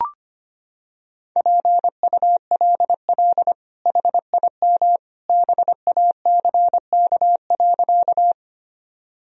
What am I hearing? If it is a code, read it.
Morse code, heard as a keyed tone, about 25 wpm, 700 Hz. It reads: PULL HIM BACK.
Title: Sentences from Top 1000 Words - Review Part 2 - 25wpm